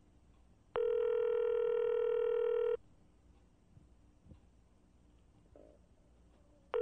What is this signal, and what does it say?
Ringback tone (North America), 440 + 480 Hz.